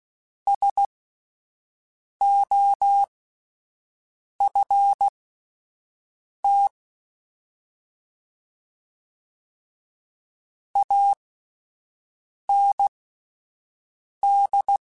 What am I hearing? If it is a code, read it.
Morse code, heard as a keyed tone, about 16 wpm, spaced as for 5 wpm, 800 Hz. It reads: SOFT AND
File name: Spaced / Chuckles Spaced-mp3s / SoftWords sp @10